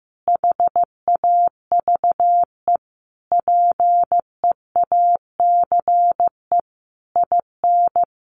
Morse code, heard as a keyed tone, about 15 wpm, 700 Hz. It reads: HAVE PEACE IN
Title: Sentences from Top 1500 Words 15wpm